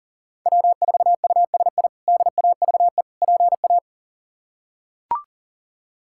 Morse code, heard as a keyed tone, about 40 wpm, 700 Hz. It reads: W4USI DAVE PA
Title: K1USN SST QSO Practice - 40wpm